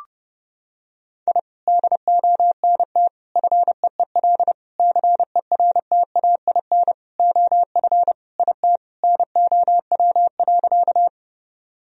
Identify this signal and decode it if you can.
Morse code, heard as a keyed tone, about 30 wpm, 700 Hz. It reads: I DONT FEEL CERTAIN OF IT NOW.
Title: Sentences from Top 1000 Words - Review Part 2 - 30wpm